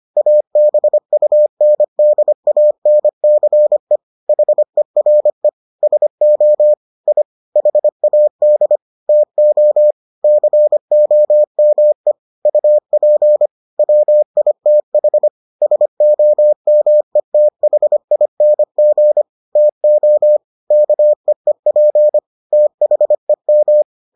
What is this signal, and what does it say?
Morse code, heard as a keyed tone, about 25 wpm, 600 Hz. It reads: ABUNDANCE HERE SO I HAD TO COME UP WITH SOMETHING TO KEEP THEM